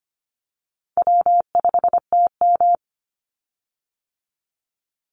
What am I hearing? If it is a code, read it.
Morse code, heard as a keyed tone, about 25 wpm, 700 Hz. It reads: W5TM
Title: Call Signs - 25wpm